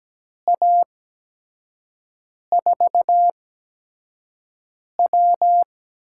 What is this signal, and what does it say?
Morse code, heard as a keyed tone, about 17 wpm, spaced as for 5 wpm, 700 Hz. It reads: A4W
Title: USA Call Signs - 8x Character Spacing - 17wpm